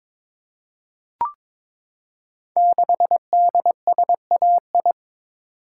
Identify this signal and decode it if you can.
Morse code, heard as a keyed tone, about 23 wpm, 700 Hz. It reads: 6DSAI
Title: Code Groups - 5 Letter:Numbers - 22wpm